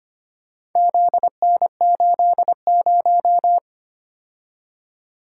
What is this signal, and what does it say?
Morse code, heard as a keyed tone, about 25 wpm, 700 Hz. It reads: ZN80
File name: practice / 25wpm / Four Letter-Number Combinations - 25wpm